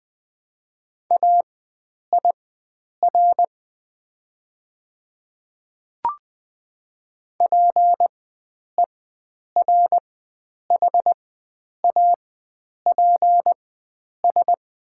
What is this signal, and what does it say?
Morse code, heard as a keyed tone, about 20 wpm, spaced as for 9 wpm, 700 Hz. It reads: AIR PERHAPS
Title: 4x Character Spacing - Top 500 Words - 20wpm